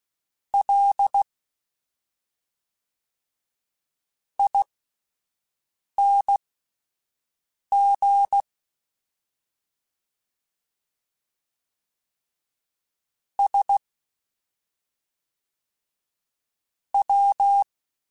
Morse code, heard as a keyed tone, about 16 wpm, spaced as for 5 wpm, 800 Hz. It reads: L ING S W
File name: Misc / Misc-mp3s / Jumps @5